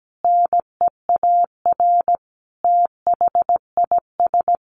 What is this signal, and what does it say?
Morse code, heard as a keyed tone, about 17 wpm, 700 Hz. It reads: NEAR THIS
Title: Sentences from Top 1400 Words 17wpm